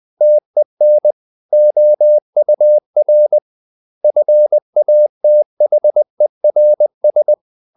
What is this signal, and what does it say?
Morse code, heard as a keyed tone, about 20 wpm, 600 Hz. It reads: TEN OUR FATHERS
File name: the old man and the sea / the old man and the sea part 1 20wpm